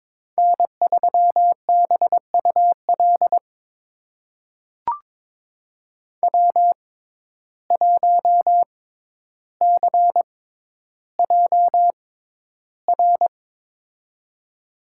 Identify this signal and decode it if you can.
Morse code, heard as a keyed tone, about 22 wpm, spaced as for 11 wpm, 700 Hz. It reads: N3BUL W1CJR
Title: USA Call Signs - 5 Characters - 6x Character Spacing - 22wpm